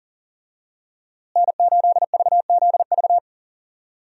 Morse code, heard as a keyed tone, about 39 wpm, 700 Hz. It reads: N8VZV
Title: USA Call Signs - 5 Characters - Repeated 3x - 40wpm